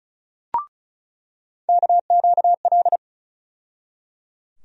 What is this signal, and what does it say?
Morse code, heard as a keyed tone, about 34 wpm, 700 Hz. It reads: KQL
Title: USA Call Sign Postfixes - 3 Letters 35wpm